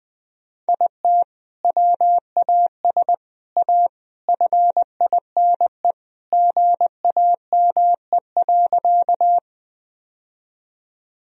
Morse code, heard as a keyed tone, about 20 wpm, 700 Hz. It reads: IT WAS A FINE GAME.
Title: Sentences from Top 1000 Words - Review Part 3 - 20wpm